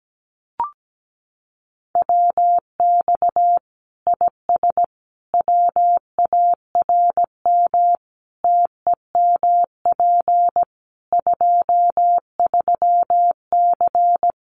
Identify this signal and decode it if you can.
Morse code, heard as a keyed tone, about 17 wpm, 700 Hz. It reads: WX IS WARM TEMP 23C
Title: Common QSO Phrases 17wpm